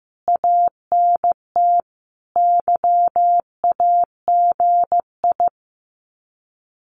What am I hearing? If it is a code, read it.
Morse code, heard as a keyed tone, about 15 wpm, 700 Hz. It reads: ANT YAGI
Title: Common QSO Phrases Encore 15wpm